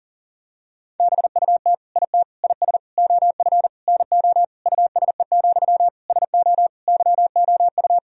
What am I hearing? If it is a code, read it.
Morse code, heard as a keyed tone, about 40 wpm, 700 Hz. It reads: BUT IT IS OF NO USE, SO YOU
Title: Sentences from Top 1000 Words - Review Part 2 - 40wpm